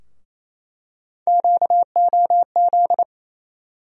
Morse code, heard as a keyed tone, about 28 wpm, 700 Hz. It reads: QOZ
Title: USA Call Sign Postfixes - 3 Letters 28wpm